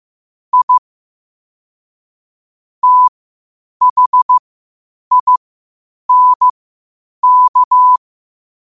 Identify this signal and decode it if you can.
Morse code, heard as a keyed tone, about 15 wpm, spaced as for 8 wpm, 1000 Hz. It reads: I THINK
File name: qotdfiles / 8wpm